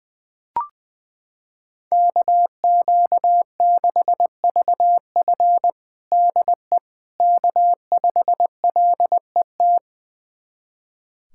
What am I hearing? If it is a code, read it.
Morse code, heard as a keyed tone, about 20 wpm, 700 Hz. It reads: KQ6VF DE K5LET
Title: USA Callsign DE Callsign - 5 Characters - 20wpm